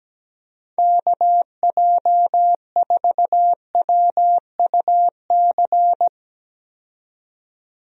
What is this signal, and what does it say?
Morse code, heard as a keyed tone, about 17 wpm, 700 Hz. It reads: KJ4WUC
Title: USA Call Signs - 6 Characters - 17wpm